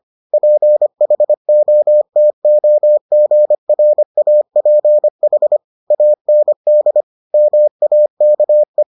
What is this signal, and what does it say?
Morse code, heard as a keyed tone, about 25 wpm, 600 Hz. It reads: PHOTOGRAPH AND MAKE